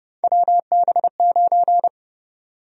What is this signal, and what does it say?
Morse code, heard as a keyed tone, about 30 wpm, 700 Hz. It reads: WB9
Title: USA Call Sign Prefixes - 30wpm